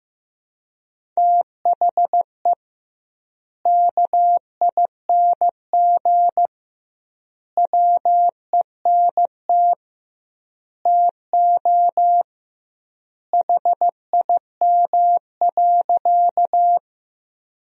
Morse code, heard as a keyed tone, about 16 wpm, 700 Hz. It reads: THE KING WENT TO HIM.
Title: 2x Word-Spacing - Sentences from Top 400 Words - 15wpm